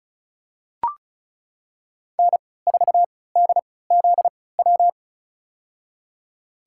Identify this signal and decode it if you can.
Morse code, heard as a keyed tone, about 34 wpm, spaced as for 20 wpm, 700 Hz. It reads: N4DZW
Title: USA Call Signs - 3x Character Spacing - 35wpm